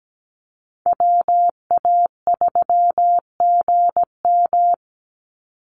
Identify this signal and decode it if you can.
Morse code, heard as a keyed tone, about 17 wpm, 700 Hz. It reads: WA3GM
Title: Call Signs - 17wpm